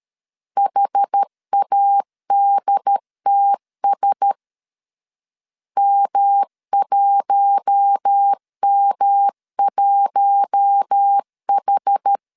Morse code, heard as a keyed tone, about 13 wpm, 800 Hz. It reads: HADTS M1M1H
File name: CW / DET / 1 6 1500C S B 12wpm 3431